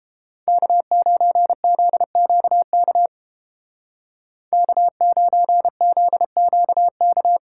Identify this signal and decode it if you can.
Morse code, heard as a keyed tone, about 29 wpm, 700 Hz. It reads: K9ZQK K9ZQK